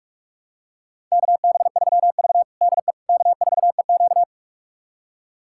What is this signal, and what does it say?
Morse code, heard as a keyed tone, about 46 wpm, 700 Hz. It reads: KB3V DE K4EQ
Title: USA Callsign DE Callsign - 4 Characters - 45wpm